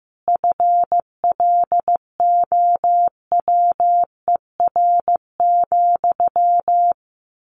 Morse code, heard as a keyed tone, about 14 wpm, 700 Hz. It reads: FLOWER,
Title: A Sound Of Thunder 15wpm